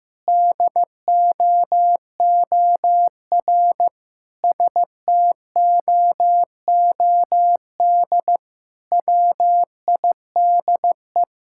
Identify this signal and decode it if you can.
Morse code, heard as a keyed tone, about 15 wpm, 700 Hz. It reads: DOOR STOOD WIDE
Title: Sentences from Top 1000 Words - Review Part 6 - 15wpm